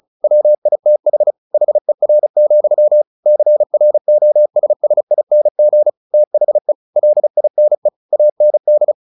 Morse code, heard as a keyed tone, about 34 wpm, 600 Hz. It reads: WITH HER, CROSSING THE LINE AND
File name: the old man and the sea / the old man and the sea 1 35wpm